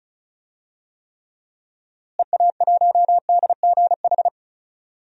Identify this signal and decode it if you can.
Morse code, heard as a keyed tone, about 36 wpm, 700 Hz. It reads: EA1DGH
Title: DXCC-100 Call Signs - 6 Characters - Repeated 3x - 35wpm